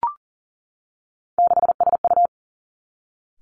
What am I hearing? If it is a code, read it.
Morse code, heard as a keyed tone, about 38 wpm, 690 Hz. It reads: 6SU
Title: Three Challenging Letter-Number Combinations 40wpm